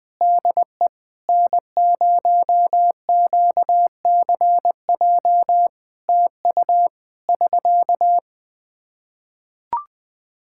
Morse code, heard as a keyed tone, about 20 wpm, 700 Hz. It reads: DE N0QCJ TU <SK>
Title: Practice QSO - Exchange 4 - Ending - 20wpm